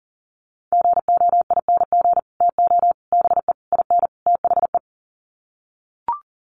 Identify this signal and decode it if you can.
Morse code, heard as a keyed tone, about 40 wpm, 700 Hz. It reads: GOING TO BE IN THE
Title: Sets of 5 Words Encore 40wpm